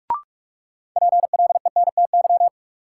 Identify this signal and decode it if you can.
Morse code, heard as a keyed tone, about 46 wpm, 700 Hz. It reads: PLENTY
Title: Sentences from Top 1300 Words - New Word Drill Alt - 45wpm